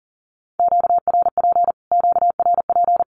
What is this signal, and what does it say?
Morse code, heard as a keyed tone, about 40 wpm, 700 Hz. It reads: QRP QRP
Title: Q-Codes Repeated 3x 40wpm